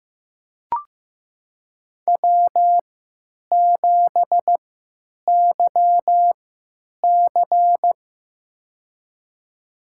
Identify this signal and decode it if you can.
Morse code, heard as a keyed tone, about 15 wpm, spaced as for 9 wpm, 700 Hz. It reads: W7YC
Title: USA Call Signs - 4 Characters - 3x Character Spacing - 15wpm